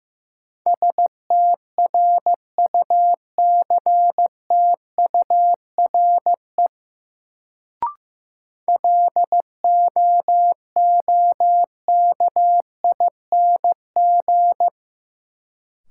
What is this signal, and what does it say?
Morse code, heard as a keyed tone, about 15 wpm, 700 Hz. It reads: STRUCTURE LOOKING
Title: Sentences from Top 1000 Words - Review Part 6 - New Word Drill - 15wpm